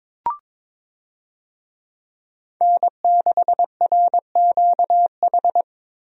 Morse code, heard as a keyed tone, about 22 wpm, 700 Hz. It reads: N6RQH
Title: USA Call Signs - 5 Characters - Repeated 3x - 22wpm